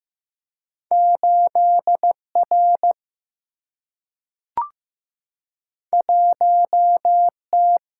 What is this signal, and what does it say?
Morse code, heard as a keyed tone, about 15 wpm, 700 Hz. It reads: 8R 1T
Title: Single-Letter Single-Number 15wpm